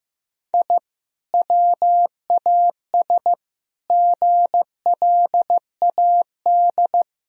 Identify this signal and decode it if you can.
Morse code, heard as a keyed tone, about 15 wpm, 700 Hz. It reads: I WAS GLAD
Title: Sentences from Top 1000 Words - Review Part 1 - 15wpm